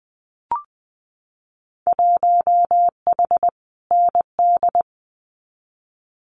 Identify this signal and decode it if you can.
Morse code, heard as a keyed tone, about 20 wpm, 700 Hz. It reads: 1H ND
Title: Winter Field Day Exchanges 20wpm